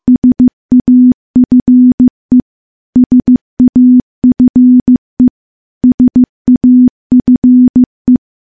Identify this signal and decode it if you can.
Morse code, heard as a keyed tone, about 15 wpm, 250 Hz. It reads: SAFE SAFE SAFE
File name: morse-training-15WPM